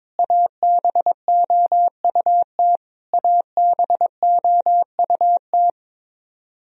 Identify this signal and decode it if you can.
Morse code, heard as a keyed tone, about 21 wpm, 700 Hz. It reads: ABOUT ABOUT
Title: Top 100 Words Repeated 22wpm